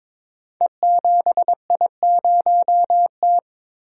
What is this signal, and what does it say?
Morse code, heard as a keyed tone, about 22 wpm, 700 Hz. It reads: E7I0T
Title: Code Groups - 5 Letter:Numbers - 22wpm